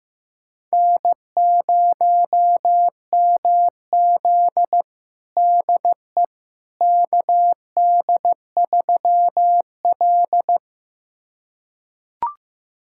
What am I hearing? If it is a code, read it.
Morse code, heard as a keyed tone, about 16 wpm, 700 Hz. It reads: N0MZ DE KD3L
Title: USA Callsign DE Callsign - 4 Characters - 15wpm